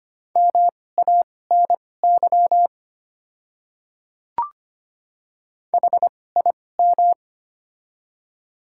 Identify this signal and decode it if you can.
Morse code, heard as a keyed tone, about 25 wpm, spaced as for 18 wpm, 700 Hz. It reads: MANY HIM
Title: 2x Character Spacing - Top 300 Words - 25wpm